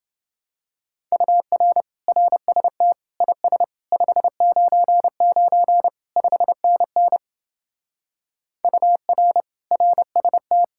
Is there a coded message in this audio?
Yes: Morse code from 1.1 to 10.6 s, heard as a keyed tone, about 29 wpm, 700 Hz. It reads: UR RST IS 599 5NN UR RST